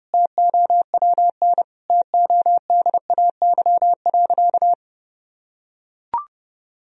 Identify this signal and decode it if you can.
Morse code, heard as a keyed tone, about 30 wpm, 700 Hz. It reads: TOWN TODAY.
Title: Sentences from Top 1000 Words - Review Part 3 - 30wpm